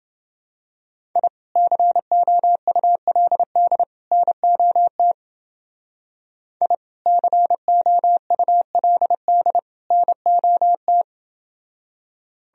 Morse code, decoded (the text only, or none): I COULD NOT I COULD NOT